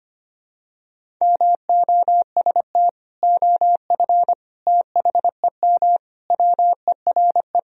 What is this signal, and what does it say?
Morse code, heard as a keyed tone, about 25 wpm, 700 Hz. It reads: MOST OF THEM WERE
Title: Sentences from Top 1000 Words - Review Part 1 - 25wpm